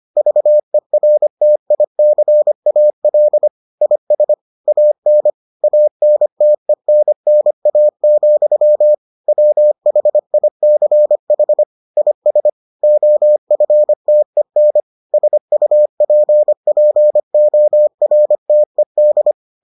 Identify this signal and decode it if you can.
Morse code, heard as a keyed tone, about 25 wpm, 600 Hz. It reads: VERTICAL IS AN ANTENNA, WHICH IS OFTEN SUPPORTED